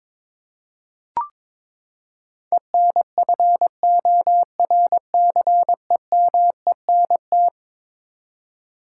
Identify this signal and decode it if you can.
Morse code, heard as a keyed tone, about 21 wpm, 700 Hz. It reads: ENFORCEMENT
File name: practice / 22wpm / Eleven Letter Words - 22wpm